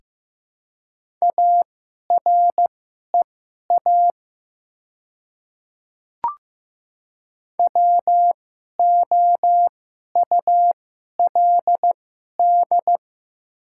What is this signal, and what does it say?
Morse code, heard as a keyed tone, about 15 wpm, spaced as for 11 wpm, 700 Hz. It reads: AREA WOULD
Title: 2x Character Spacing - Top 500 Words - 15wpm